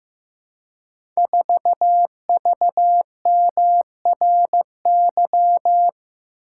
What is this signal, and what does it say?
Morse code, heard as a keyed tone, about 15 wpm, 700 Hz. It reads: 4VMRY
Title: Code Groups - 5 Letter:Numbers - 15wpm